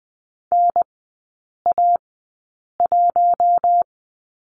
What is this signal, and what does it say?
Morse code, heard as a keyed tone, about 20 wpm, spaced as for 8 wpm, 700 Hz. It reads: NA1
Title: 041 Lesson 8e - Number 1 - Three Characters 20wpm